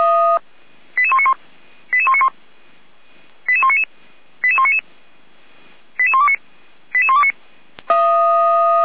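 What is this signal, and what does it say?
Five-tone selective call (ZVEI), heard as quick runs of five tones, tones of about 70 ms. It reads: 88191 88191 88199 88199 88129 88129
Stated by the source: ZVEI five-tone selective call 88288, received by SDR.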